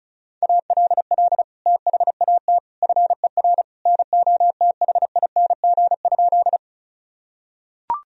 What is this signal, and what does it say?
Morse code, heard as a keyed tone, about 34 wpm, 700 Hz. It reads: ALL THAT FER NOTHING?
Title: Sentences from Top 500 Words and Abbreviations 35wpm